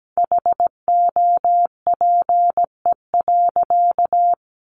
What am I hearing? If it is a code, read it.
Morse code, heard as a keyed tone, about 17 wpm, 700 Hz. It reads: HOPE.
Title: Sentences from Top 1400 Words 17wpm